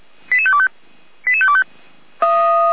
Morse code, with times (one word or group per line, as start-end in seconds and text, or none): none